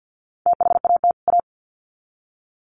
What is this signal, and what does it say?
Morse code, heard as a keyed tone, about 48 wpm, 700 Hz. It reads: THAT A
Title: Sets of 2 Words 50wpm